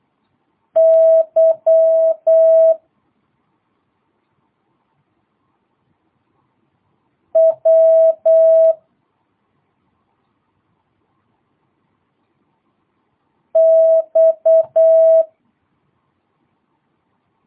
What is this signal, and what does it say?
Morse code, heard as a keyed tone, about 8 wpm, 650 Hz. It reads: Y W X
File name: jh8cbh / 2022cw